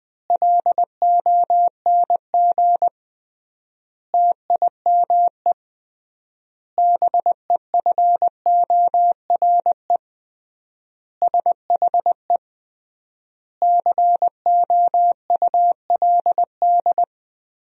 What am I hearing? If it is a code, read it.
Morse code, heard as a keyed tone, about 20 wpm, 700 Hz. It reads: LONG TIME BEFORE SHE COULD
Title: 3x Word-Spacing - Sentences from Top 500 Words - 20wpm